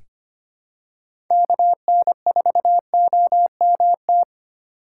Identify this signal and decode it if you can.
Morse code, heard as a keyed tone, about 25 wpm, 700 Hz. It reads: KN4OMT
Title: USA Call Signs - 6 Characters - 8x Character Spacing - 25wpm